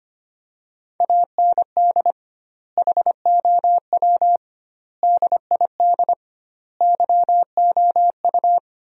Morse code, heard as a keyed tone, about 25 wpm, 700 Hz. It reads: AND HOW DID YOU
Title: 2x Word-Spacing - Sentences from Top 100 Words - 25wpm